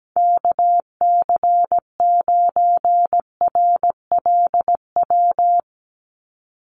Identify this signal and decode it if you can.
Morse code, heard as a keyed tone, about 17 wpm, 700 Hz. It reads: KC9RLW
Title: USA Call Signs 17wpm